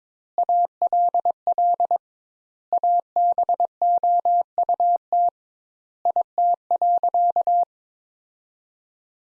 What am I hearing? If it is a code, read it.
Morse code, heard as a keyed tone, about 22 wpm, 700 Hz. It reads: ALL ABOUT IT.
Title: Sentences Top 200 Words - 22wpm